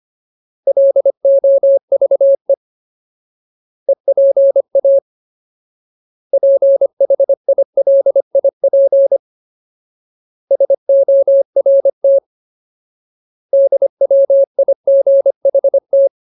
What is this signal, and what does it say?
Morse code, heard as a keyed tone, about 25 wpm, 550 Hz. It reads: LOVE EPA PHILIP SORT DWIGHT
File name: Ultimate Word List / 25WPM Ultimate